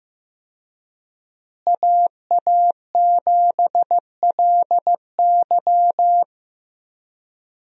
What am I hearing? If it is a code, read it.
Morse code, heard as a keyed tone, about 15 wpm, 700 Hz. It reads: AA7LY
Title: USA Call Signs - 5 Characters - Repeated 3x - 15wpm